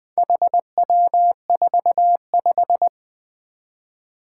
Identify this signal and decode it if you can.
Morse code, heard as a keyed tone, about 20 wpm, 700 Hz. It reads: HW45
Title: Four Challenging Letter-Number Combinations 20wpm